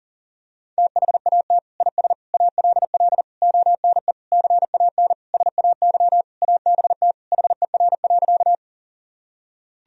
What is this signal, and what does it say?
Morse code, heard as a keyed tone, about 40 wpm, 700 Hz. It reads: THAT IS ALL ONE CAN SAY ABT HER.
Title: Sentences from Top 200 Words and Abbreviations 40wpm